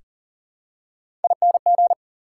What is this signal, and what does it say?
Morse code, heard as a keyed tone, about 40 wpm, 700 Hz. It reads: ING
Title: Prefixes and Suffixes 40wpm